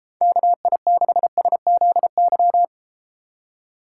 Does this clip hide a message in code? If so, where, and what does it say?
Yes: Morse code from 0.2 to 2.6 s, heard as a keyed tone, about 32 wpm, 700 Hz. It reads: KI6SZY